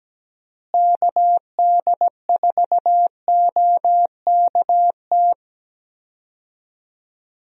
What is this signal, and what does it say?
Morse code, heard as a keyed tone, about 17 wpm, 700 Hz. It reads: KD4OKT
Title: USA Call Signs - 6 Characters - Repeated 3x - 17wpm